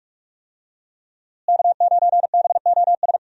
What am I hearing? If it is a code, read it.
Morse code, heard as a keyed tone, about 46 wpm, 700 Hz. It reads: K9BOS
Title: USA Call Signs - 5 Characters - Repeated 3x - 45wpm